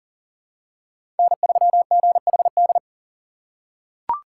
Morse code, heard as a keyed tone, about 40 wpm, 700 Hz. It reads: N3GHD
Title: USA Call Signs - 5 Characters - 2x Character Spacing - 40wpm